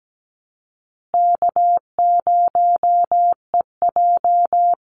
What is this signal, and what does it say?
Morse code, heard as a keyed tone, about 17 wpm, 700 Hz. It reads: K0EJ
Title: Call Signs Repeated 2x - 17wpm